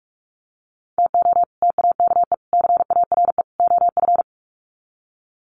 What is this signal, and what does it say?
Morse code, heard as a keyed tone, about 44 wpm, 700 Hz. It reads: TO TAKE CARE OF
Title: Sets of 4 Words 45wpm